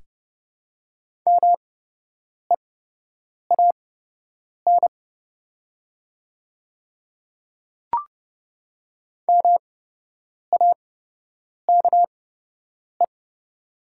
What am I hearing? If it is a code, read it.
Morse code, heard as a keyed tone, about 30 wpm, spaced as for 8 wpm, 700 Hz. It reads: MEAN MAKE
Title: 8x Character Spacing - Top 200 Words - 30wpm